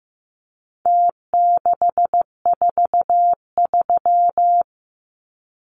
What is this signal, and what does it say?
Morse code, heard as a keyed tone, about 15 wpm, 700 Hz. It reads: T643
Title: Numbers - Contest Style - 4-Digit 15wpm